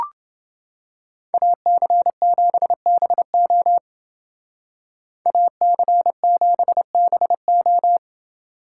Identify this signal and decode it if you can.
Morse code, heard as a keyed tone, about 28 wpm, 700 Hz. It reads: AC7BO AC7BO